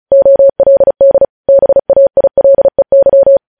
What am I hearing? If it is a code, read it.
Morse code, heard as a keyed tone, about 36 wpm, 550 Hz. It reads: OLD BAILEY
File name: podcast / News headlines-1x-35x35